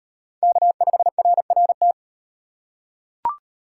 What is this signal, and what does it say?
Morse code, heard as a keyed tone, about 38 wpm, 700 Hz. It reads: K5RRT